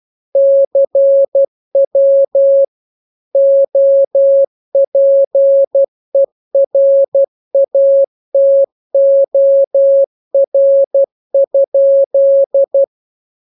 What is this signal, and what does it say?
Morse code, heard as a keyed tone, about 12 wpm, 550 Hz. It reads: CW OPERATOR?